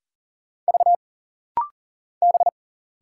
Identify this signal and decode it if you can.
Morse code, heard as a keyed tone, about 40 wpm, 700 Hz. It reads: V B
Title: Single Letter-Number Alt - Rapid-Fire 40wpm